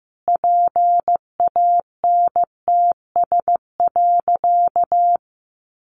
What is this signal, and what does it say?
Morse code, heard as a keyed tone, about 15 wpm, 700 Hz. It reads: PANTS.
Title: A Sound Of Thunder 15wpm